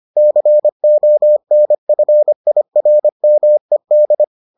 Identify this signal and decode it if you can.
Morse code, heard as a keyed tone, about 25 wpm, 600 Hz. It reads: CONFIRMED